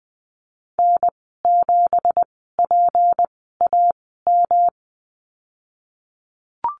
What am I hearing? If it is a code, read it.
Morse code, heard as a keyed tone, about 20 wpm, spaced as for 15 wpm, 700 Hz. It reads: N7PAM
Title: 167 Lesson 27g - Number 7 - Call Signs 20wpm